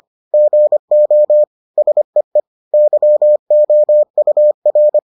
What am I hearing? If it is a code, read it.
Morse code, heard as a keyed tone, about 25 wpm, 600 Hz. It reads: GO SEE YOUR